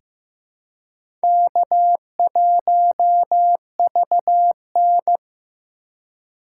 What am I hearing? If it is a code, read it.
Morse code, heard as a keyed tone, about 15 wpm, 700 Hz. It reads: K1VN
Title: USA Call Signs - 4 Characters - 3x Character Spacing - 15wpm